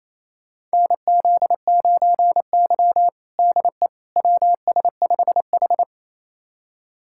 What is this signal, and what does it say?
Morse code, heard as a keyed tone, about 27 wpm, 700 Hz. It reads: NZ9Y DE WS5H
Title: USA Callsign DE Callsign - 4 Characters - 28wpm